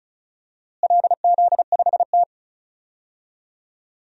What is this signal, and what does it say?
Morse code, heard as a keyed tone, about 36 wpm, 700 Hz. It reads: LZ5T
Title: DXCC-100 Call Signs - 4 Characters - Repeated 2x - 35wpm